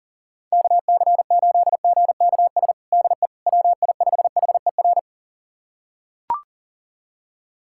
Morse code, heard as a keyed tone, about 37 wpm, 700 Hz. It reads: KC8GKS DE WI5HER
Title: USA Callsign DE Callsign 40wpm